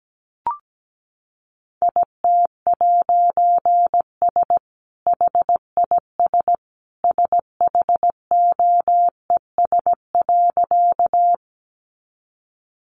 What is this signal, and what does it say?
Morse code, heard as a keyed tone, about 17 wpm, 700 Hz. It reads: IT'S HIS SHOES.
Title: A Sound Of Thunder 17wpm